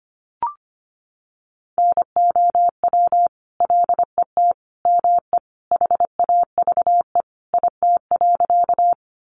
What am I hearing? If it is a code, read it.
Morse code, heard as a keyed tone, about 25 wpm, 700 Hz. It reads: NOW LET ME HAVE IT.
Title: Sentences from Top 300 Words 25wpm